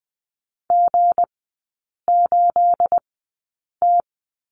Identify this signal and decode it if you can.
Morse code, heard as a keyed tone, about 20 wpm, spaced as for 8 wpm, 690 Hz. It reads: G8T
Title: 212 Lesson 34e - Number 8 - Three Characters 20wpm